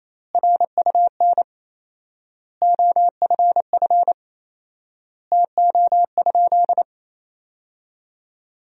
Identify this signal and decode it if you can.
Morse code, heard as a keyed tone, about 28 wpm, 700 Hz. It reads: RUN OFF TO?
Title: 4x Word-Spacing - Sentences from Top 300 Words - 28wpm